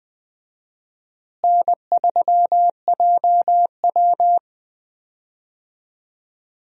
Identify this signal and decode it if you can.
Morse code, heard as a keyed tone, about 20 wpm, 700 Hz. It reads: N3JW
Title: USA Call Signs - 4 Characters - Repeated 3x - 20wpm